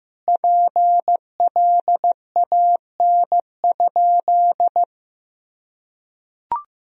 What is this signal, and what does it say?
Morse code, heard as a keyed tone, about 15 wpm, 700 Hz. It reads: PLAN?
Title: Sentences from Top 500 Words and Abbreviations 15wpm